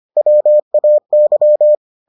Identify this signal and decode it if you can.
Morse code, heard as a keyed tone, about 25 wpm, 600 Hz. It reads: WAY